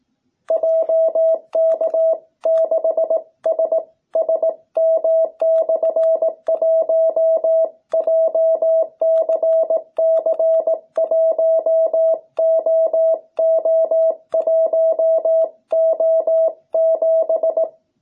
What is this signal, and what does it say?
Morse code, heard as a keyed tone, about 18 wpm, 660 Hz. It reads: JK6SSM/1 JCC1OO1O7